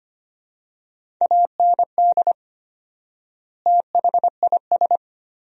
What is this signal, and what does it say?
Morse code, heard as a keyed tone, about 25 wpm, 700 Hz. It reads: AND THIS
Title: 4x Word-Spacing - Sentences from Top 300 Words - 25wpm